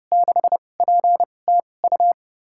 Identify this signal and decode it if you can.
Morse code, heard as a keyed tone, about 30 wpm, spaced as for 22 wpm, 700 Hz. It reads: 6PTU
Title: USA Call Signs - 2x Character Spacing - 30wpm